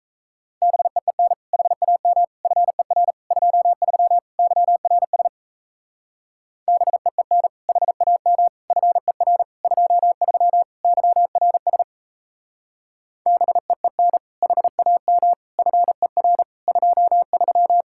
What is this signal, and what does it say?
Morse code, heard as a keyed tone, about 37 wpm, 700 Hz. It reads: BEEN HAM FER 23 YRS BEEN HAM FER 23 YRS BEEN HAM FER 23